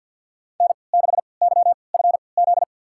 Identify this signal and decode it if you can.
Morse code, heard as a keyed tone, about 48 wpm, spaced as for 28 wpm, 700 Hz. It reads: N6YFZ
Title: USA Call Signs - 3x Character Spacing - 50wpm